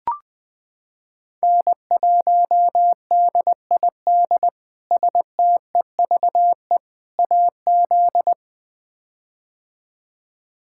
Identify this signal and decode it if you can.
Morse code, heard as a keyed tone, about 20 wpm, 700 Hz. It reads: N1DID STEVE AZ
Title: K1USN SST QSO Practice - 20wpm